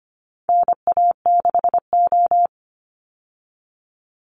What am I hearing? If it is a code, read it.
Morse code, heard as a keyed tone, about 25 wpm, 700 Hz. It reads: NA6O
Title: Call Signs Repeated 2x - 25wpm